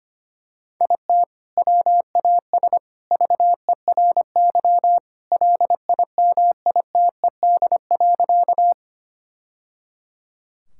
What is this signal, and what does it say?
Morse code, heard as a keyed tone, about 25 wpm, 700 Hz. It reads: IT WAS VERY LIMITED.
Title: Sentences from Top 1000 Words - Review Part 6 - 25wpm